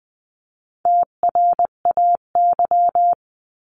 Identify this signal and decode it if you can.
Morse code, heard as a keyed tone, about 20 wpm, 700 Hz. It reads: TRAY
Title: 259 Lesson 42d - Words 18-20wpm